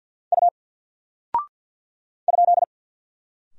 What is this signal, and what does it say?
Morse code, heard as a keyed tone, about 49 wpm, 700 Hz. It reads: U ?
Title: Single Character (Letter:Number:Punctuation:Prosign) Alt 50wpm